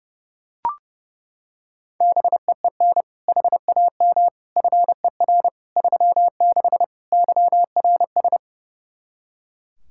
Morse code, heard as a keyed tone, about 31 wpm, 700 Hz. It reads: BEEN HAM FER 36 YRS
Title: Practice QSO - Exchange 3 - More Info - 30wpm